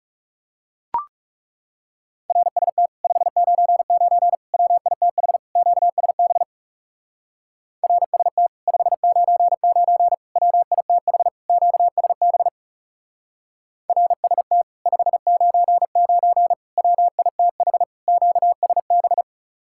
Morse code, heard as a keyed tone, about 40 wpm, 700 Hz. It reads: RST 599 WITH QSB RST 599 WITH QSB RST 599 WITH QSB